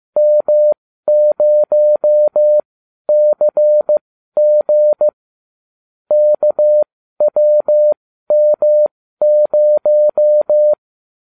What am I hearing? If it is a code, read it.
Morse code, heard as a keyed tone, about 15 wpm, spaced as for 12 wpm, 600 Hz. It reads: M0CG KWM0